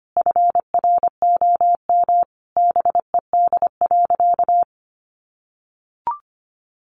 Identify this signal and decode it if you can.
Morse code, heard as a keyed tone, about 25 wpm, 700 Hz. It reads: FROM BED.
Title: Sentences from Top 900 Words 25wpm